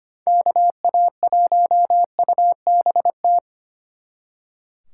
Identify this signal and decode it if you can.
Morse code, heard as a keyed tone, about 29 wpm, 700 Hz. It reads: KA1UBT
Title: Speed Racing - USA 2x3 Call Signs 25wpm